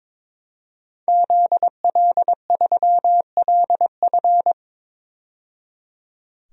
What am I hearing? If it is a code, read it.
Morse code, heard as a keyed tone, about 22 wpm, 700 Hz. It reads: ZL3LF
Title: DXCC-100 Call Signs - 5 Characters - 22wpm